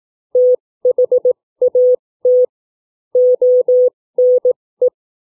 Morse code, heard as a keyed tone, about 18 wpm, spaced as for 15 wpm, 500 Hz. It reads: THAT ONE
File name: CW / book reading